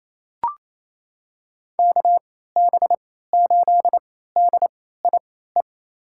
Morse code, heard as a keyed tone, about 28 wpm, spaced as for 16 wpm, 700 Hz. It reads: KB8DIE
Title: USA Call Signs - 6 Characters - 3x Character Spacing - 28wpm